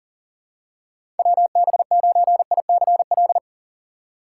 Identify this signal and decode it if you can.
Morse code, heard as a keyed tone, about 40 wpm, 700 Hz. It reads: WB9ICL
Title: USA Call Signs - 6x Character Spacing - 40wpm